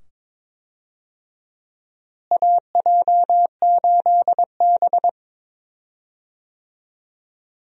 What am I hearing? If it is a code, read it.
Morse code, heard as a keyed tone, about 22 wpm, 700 Hz. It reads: AJ8B
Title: USA Call Signs - 4 Characters - Repeated 2x - 22wpm